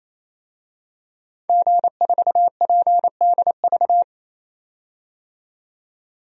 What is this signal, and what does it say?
Morse code, heard as a keyed tone, about 28 wpm, 700 Hz. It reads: G4PDV
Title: DXCC-100 Call Signs - 5 Characters - 28wpm